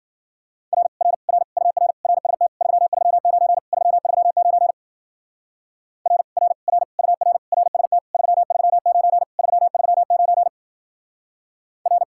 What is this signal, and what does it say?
Morse code, heard as a keyed tone, about 49 wpm, 700 Hz. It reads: R R R UR RST 339 339 R R R UR RST 339 339 R
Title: Speed-Racing Common QSO Phrases - 40wpm